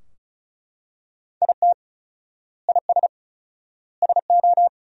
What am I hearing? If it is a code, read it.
Morse code, heard as a keyed tone, about 36 wpm, 700 Hz. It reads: IT IS SO
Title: 4x Word-Spacing - Sentences from Top 300 Words - 35wpm